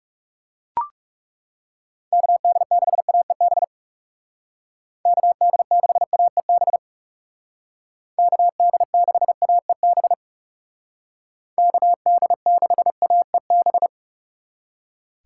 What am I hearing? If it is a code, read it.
Morse code, heard as a keyed tone, about 39 wpm, 700 Hz. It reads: KD6AEB KD6AEB KD6AEB KD6AEB